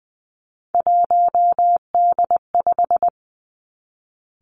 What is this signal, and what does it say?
Morse code, heard as a keyed tone, about 20 wpm, 700 Hz. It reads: 1D5
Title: Three Challenging Letter-Number Combinations 20wpm